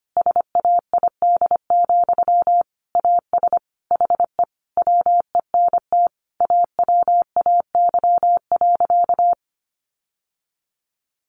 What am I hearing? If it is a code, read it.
Morse code, heard as a keyed tone, about 25 wpm, 700 Hz. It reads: SAID, AS HE WENT AWAY.
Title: Sentences from Top 600 Words 25wpm